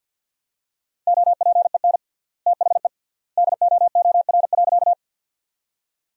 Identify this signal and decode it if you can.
Morse code, heard as a keyed tone, about 51 wpm, 700 Hz. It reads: OPEN THE DOOR.
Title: 3x Word-Spacing - Sentences from Top 300 Words - 50wpm